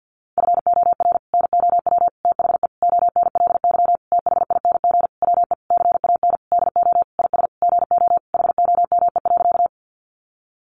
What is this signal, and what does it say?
Morse code, heard as a keyed tone, about 50 wpm, 700 Hz. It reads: FOR NOW THE ONLY THING WE CAN DO IS GO HOME.
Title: Sentences from Top 200 Words 50wpm